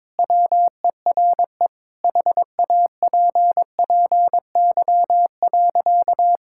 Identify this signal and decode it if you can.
Morse code, heard as a keyed tone, about 22 wpm, 700 Hz. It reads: WERE HAPPY.
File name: practice / 22wpm / Sentences with 2-Word Phrases - 22wpm